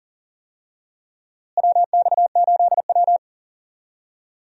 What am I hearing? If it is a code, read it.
Morse code, heard as a keyed tone, about 40 wpm, 700 Hz. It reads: WX8W
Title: USA Call Signs - 4 Characters - Repeated 3x - 40wpm